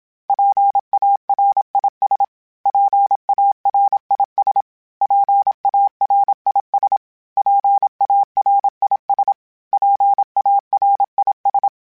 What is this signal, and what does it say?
Morse code, heard as a keyed tone, about 27 wpm, 800 Hz. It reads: PARIS PARIS PARIS PARIS PARIS
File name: PARIS@25